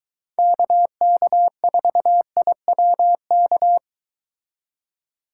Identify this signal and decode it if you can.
Morse code, heard as a keyed tone, about 23 wpm, 700 Hz. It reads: KK4IWK